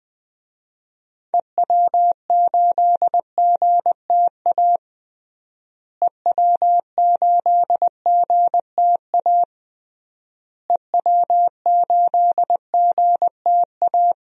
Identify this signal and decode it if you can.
Morse code, heard as a keyed tone, about 20 wpm, 700 Hz. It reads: EW8GTA EW8GTA EW8GTA
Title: DXCC-100 Call Signs - 6 Characters - Repeated 3x - 20wpm